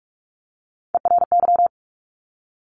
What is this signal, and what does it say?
Morse code, heard as a keyed tone, about 46 wpm, 690 Hz. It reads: ERY
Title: Top Three Letter Combinations in Words 45wpm